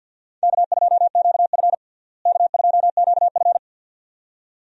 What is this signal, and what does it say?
Morse code, heard as a keyed tone, about 47 wpm, 700 Hz. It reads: K2QF K2QF
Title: USA Call Signs - 4 Characters - Repeated 2x - 50wpm